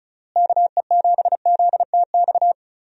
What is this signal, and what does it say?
Morse code, heard as a keyed tone, about 36 wpm, 700 Hz. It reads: KE7ZTX
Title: USA Call Signs - 6 Characters - Repeated 3x - 35wpm